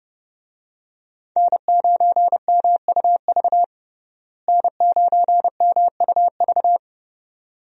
Morse code, heard as a keyed tone, about 30 wpm, 700 Hz. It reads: N9MUV N9MUV
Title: USA Call Signs - 5 Characters - Repeated 2x - 30wpm